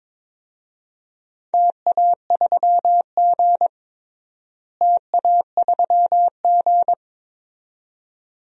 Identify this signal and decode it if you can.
Morse code, heard as a keyed tone, about 22 wpm, 700 Hz. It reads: TA3G TA3G
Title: DXCC-100 Call Signs - 4 Characters - Repeated 2x - 22wpm